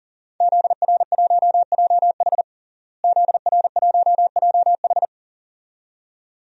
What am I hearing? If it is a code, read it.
Morse code, heard as a keyed tone, about 40 wpm, 700 Hz. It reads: ZR1JH ZR1JH
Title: DXCC-100 Call Signs - 5 Characters - Repeated 2x - 40wpm